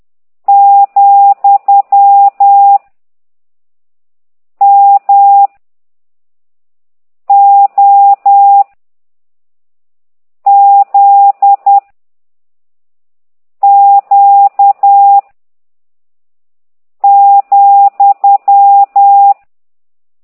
Morse code, heard as a keyed tone, about 10 wpm, 800 Hz. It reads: , M O Z Q ,